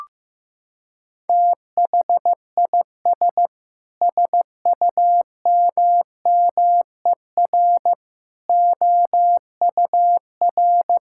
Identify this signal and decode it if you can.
Morse code, heard as a keyed tone, about 16 wpm, 700 Hz. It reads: THIS SUMMER OUR
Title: Sentences from Top 1600 Words 15wpm